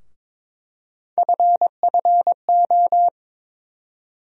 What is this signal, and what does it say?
Morse code, heard as a keyed tone, about 22 wpm, 700 Hz. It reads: FFO
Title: USA Call Sign Postfixes - 3 Letters 22wpm